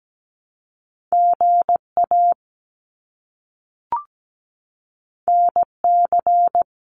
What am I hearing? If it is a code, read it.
Morse code, heard as a keyed tone, about 17 wpm, 700 Hz. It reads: GA NC
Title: US States Repeated 3x 17wpm